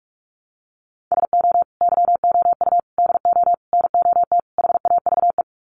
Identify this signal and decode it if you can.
Morse code, heard as a keyed tone, about 44 wpm, 700 Hz. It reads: SO YOU DO NOT HAVE
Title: Sets of 5 Words Encore 45wpm